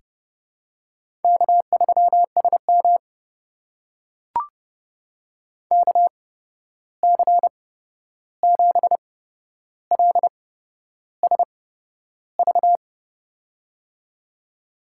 Morse code, heard as a keyed tone, about 30 wpm, spaced as for 11 wpm, 700 Hz. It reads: K3SM KC7LSV
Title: USA Call Signs - 8x Character Spacing - 30wpm